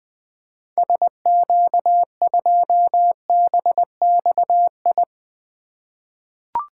Morse code, heard as a keyed tone, about 20 wpm, 700 Hz. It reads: SQ2BXI
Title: 2023 POTA Activator Call Signs - 20wpm